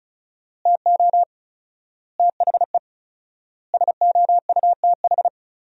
Morse code, heard as a keyed tone, about 34 wpm, 700 Hz. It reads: TO THE SOUTH
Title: 4x Word-Spacing - Sentences from Top 400 Words - 35wpm